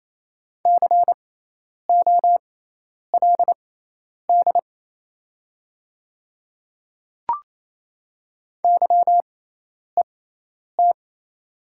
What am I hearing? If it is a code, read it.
Morse code, heard as a keyed tone, about 28 wpm, spaced as for 10 wpm, 700 Hz. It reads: COLD YET
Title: 6x Character Spacing - Top 500 Words - 28wpm